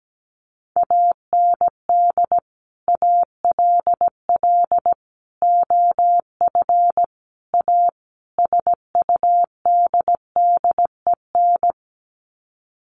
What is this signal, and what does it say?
Morse code, heard as a keyed tone, about 17 wpm, 700 Hz. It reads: AND ALL OF A SUDDEN
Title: Sets of 5 Words 17wpm